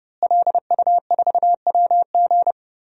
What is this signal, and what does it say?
Morse code, heard as a keyed tone, about 30 wpm, 700 Hz. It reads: LU4WG
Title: DXCC-100 Call Signs - 5 Characters - Repeated 3x - 30wpm